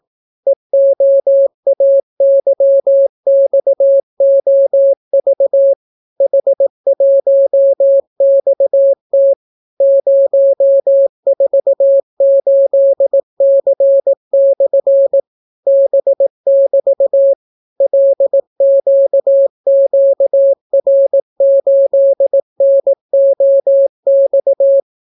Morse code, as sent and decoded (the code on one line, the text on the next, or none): . --- .- -.-- -..- --- ...- / .... .---- -..- - / ----- ....- ---.. -.-. -..-. / -... -...- / .-.. --.- --.- .-. ---.. -. --- -..-
EOAYXOV H1XT 048C/ B<BT> LQQR8NOX